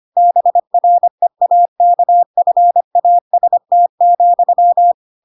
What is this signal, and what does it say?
Morse code, heard as a keyed tone, about 25 wpm, 700 Hz. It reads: BREAKFAST,
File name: Pinocchio / Pinocchio ch7 25 25 ik1hge